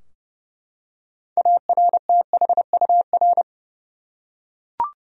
Morse code, heard as a keyed tone, about 30 wpm, 700 Hz. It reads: ARTHUR
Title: USA Common Names - 30wpm